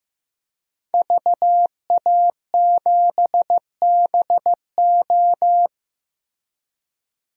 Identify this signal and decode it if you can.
Morse code, heard as a keyed tone, about 15 wpm, 700 Hz. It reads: VA7BO
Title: DXCC-100 Call Signs - 5 Characters - 15wpm